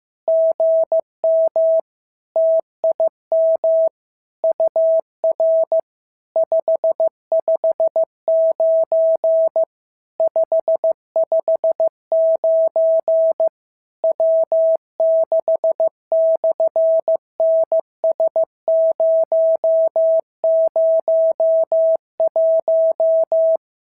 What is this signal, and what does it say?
Morse code, heard as a keyed tone, about 14 wpm, 650 Hz. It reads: GM TIM UR 559 559 W6/NS001
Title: SOTA QSO Practice 15wpm